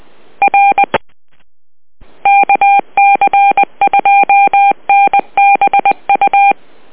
Morse code, heard as a keyed tone, about 20 wpm, 800 Hz. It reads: R KC2NBU
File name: net audio